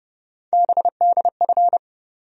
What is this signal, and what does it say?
Morse code, heard as a keyed tone, about 32 wpm, 700 Hz. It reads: BDF
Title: USA Call Sign Postfixes - 3 Letters 30wpm